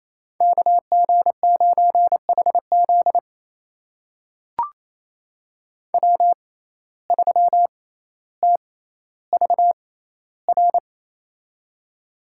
Morse code, heard as a keyed tone, about 28 wpm, spaced as for 14 wpm, 700 Hz. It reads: KG9HZ W3TVR
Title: USA Call Signs - 5 Characters - 6x Character Spacing - 28wpm